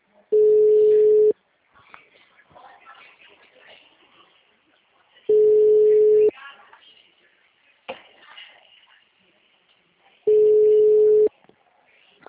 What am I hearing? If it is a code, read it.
Ringback tone (Europe, 425 Hz), with speech.